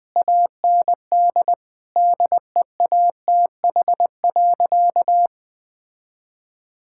Morse code, heard as a keyed tone, about 20 wpm, 700 Hz. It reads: AND DEATH.
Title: Sentences with 8-Word Phrases - 20wpm